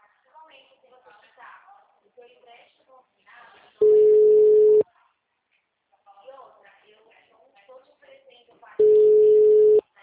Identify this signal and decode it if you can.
Ringback tone (Europe, 425 Hz), with speech.